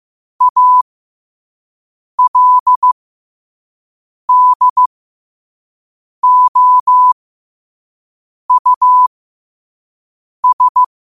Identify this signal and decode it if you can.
Morse code, heard as a keyed tone, about 15 wpm, spaced as for 5 wpm, 1000 Hz. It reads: ALDOUS
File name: qotdfiles / 5wpm